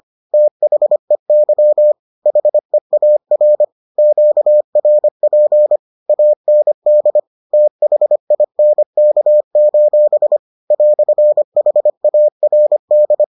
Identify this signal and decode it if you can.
Morse code, heard as a keyed tone, about 25 wpm, 600 Hz. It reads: THEY HEAR QRP AND THINK: "HARD